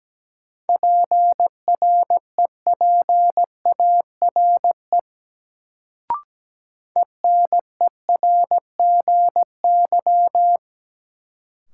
Morse code, heard as a keyed tone, about 17 wpm, 700 Hz. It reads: PREPARE ENERGY
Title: Sentences from Top 900 Words - New Word Drill - 17wpm